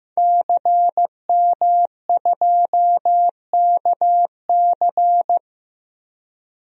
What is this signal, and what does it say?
Morse code, heard as a keyed tone, about 15 wpm, 700 Hz. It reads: CM2KC
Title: DXCC-100 Call Signs - 5 Characters - 15wpm